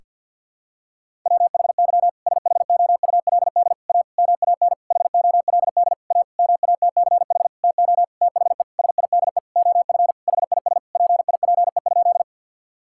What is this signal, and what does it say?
Morse code, heard as a keyed tone, about 49 wpm, 700 Hz. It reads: WHY SHOULD A MAN HOLD A MATCH TO THE SIDE OF HIS PIPE?
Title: Sentences from Top 1600 Words 50wpm